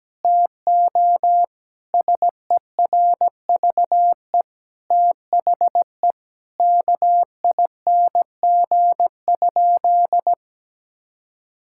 Morse code, heard as a keyed tone, about 17 wpm, 700 Hz. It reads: TO SERVE THE KING?
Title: Sentences from Top 1600 Words 17wpm